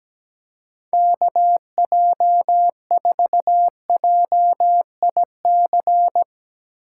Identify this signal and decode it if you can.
Morse code, heard as a keyed tone, about 17 wpm, 700 Hz. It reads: KJ4JIC
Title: USA Call Signs - 6 Characters - 4x Character Spacing - 17wpm